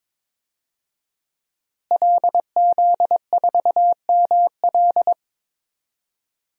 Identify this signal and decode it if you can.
Morse code, heard as a keyed tone, about 22 wpm, 700 Hz. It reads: LZ4ML
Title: DXCC-100 Call Signs - 5 Characters - Repeated 2x - 22wpm